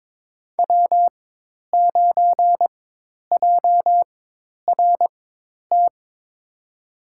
Morse code, heard as a keyed tone, about 22 wpm, spaced as for 10 wpm, 700 Hz. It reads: W9JRT
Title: USA Call Signs - 5 Characters - 4x Character Spacing - 22wpm